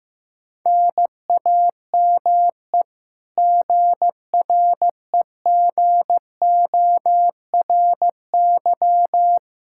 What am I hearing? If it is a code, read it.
Morse code, heard as a keyed tone, about 15 wpm, 700 Hz. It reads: NAME GREGORY
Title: Practice QSO - Exchange 1 - Beginning - 15wpm